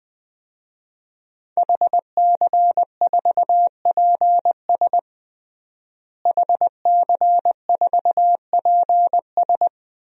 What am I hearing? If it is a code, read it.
Morse code, heard as a keyed tone, about 20 wpm, 700 Hz. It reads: HC4PS HC4PS
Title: DXCC-100 Call Signs - 5 Characters - Repeated 3x - 20wpm